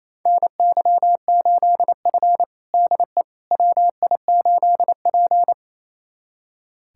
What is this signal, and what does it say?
Morse code, heard as a keyed tone, about 28 wpm, 700 Hz. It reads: NY8F DE WI8P
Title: USA Callsign DE Callsign - 4 Characters - 28wpm